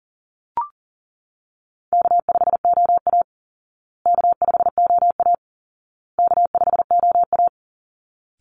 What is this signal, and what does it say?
Morse code, heard as a keyed tone, about 37 wpm, 700 Hz. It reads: K5OA K5OA K5OA
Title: Call Signs Repeated 2x - 40wpm